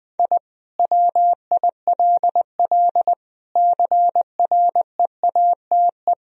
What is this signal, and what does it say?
Morse code, heard as a keyed tone, about 20 wpm, 700 Hz. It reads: I WILL CREATE
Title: Sentences from Top 1000 Words - Review Part 3 - 20wpm